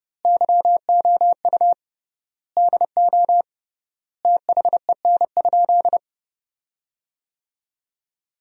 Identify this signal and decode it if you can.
Morse code, heard as a keyed tone, about 31 wpm, 700 Hz. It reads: YOU DO THEN?
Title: 3x Word-Spacing - Sentences from Top 100 Words - 30wpm